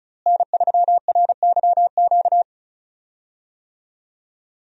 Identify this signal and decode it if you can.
Morse code, heard as a keyed tone, about 34 wpm, 700 Hz. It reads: N3RYQ
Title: USA Call Signs - 5 Characters - Repeated 2x - 35wpm